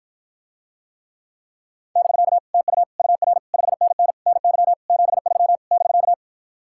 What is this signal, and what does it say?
Morse code, heard as a keyed tone, about 53 wpm, 700 Hz. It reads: <BK> TU UR 5NN NY 73 <BK>